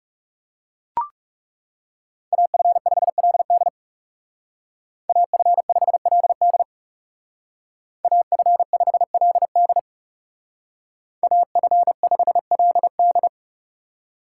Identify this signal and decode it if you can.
Morse code, heard as a keyed tone, about 39 wpm, 700 Hz. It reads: AF5LD AF5LD AF5LD AF5LD